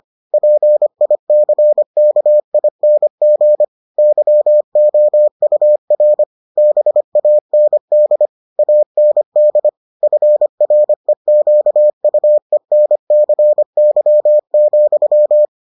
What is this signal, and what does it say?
Morse code, heard as a keyed tone, about 25 wpm, 600 Hz. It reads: PICKING YOUR BAND AND FREQUENCY,